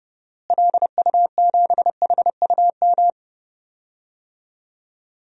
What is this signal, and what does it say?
Morse code, heard as a keyed tone, about 30 wpm, 700 Hz. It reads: LU7HUM
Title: DXCC-100 Call Signs - 6 Characters - Repeated 2x - 30wpm